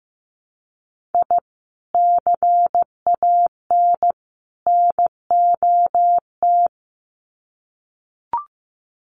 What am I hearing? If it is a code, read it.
Morse code, heard as a keyed tone, about 15 wpm, 700 Hz. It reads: I CAN NOT
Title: Sets of 3 Words 15wpm